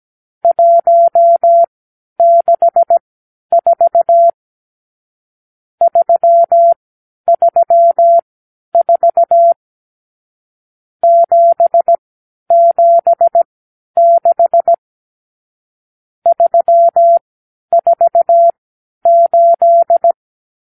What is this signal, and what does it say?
Morse code, heard as a keyed tone, about 17 wpm, spaced as for 10 wpm, 680 Hz. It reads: 164 334 776 348
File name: k7qo / Lesson 54 (9)